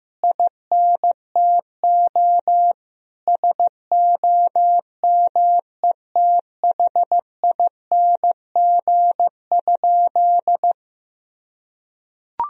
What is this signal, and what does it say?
Morse code, heard as a keyed tone, about 14 wpm, 700 Hz. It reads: INTO SOMETHING?
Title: Sentences from Top 1000 Words - Review Part 2 - 15wpm